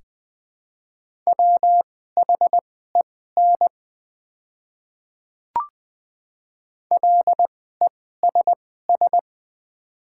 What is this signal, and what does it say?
Morse code, heard as a keyed tone, about 20 wpm, spaced as for 14 wpm, 690 Hz. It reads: WHEN LESS
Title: 2x Character Spacing - Top 500 Words - 20wpm